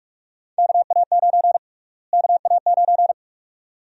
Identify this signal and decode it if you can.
Morse code, heard as a keyed tone, about 45 wpm, 700 Hz. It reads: KA9 KA9
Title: USA Call Sign Prefixes - Repeated 2x - 45wpm